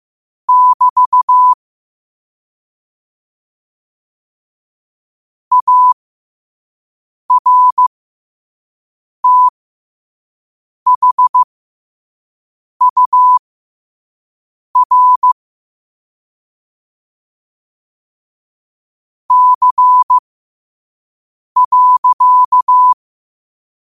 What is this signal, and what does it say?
Morse code, heard as a keyed tone, about 15 wpm, spaced as for 5 wpm, 1000 Hz. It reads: <BT> ARTHUR C.